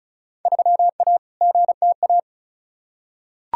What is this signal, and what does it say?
Morse code, heard as a keyed tone, about 34 wpm, 700 Hz. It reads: 3A GTA
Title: Field Day Exchanges - 35wpm